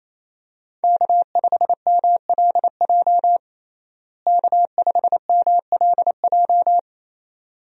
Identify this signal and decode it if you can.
Morse code, heard as a keyed tone, about 28 wpm, 700 Hz. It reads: K5MLJ K5MLJ
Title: USA Call Signs - 5 Characters - Repeated 2x - 28wpm